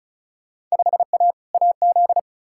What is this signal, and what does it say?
Morse code, heard as a keyed tone, about 36 wpm, 700 Hz. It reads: 5A AZ
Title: Field Day Exchanges - 35wpm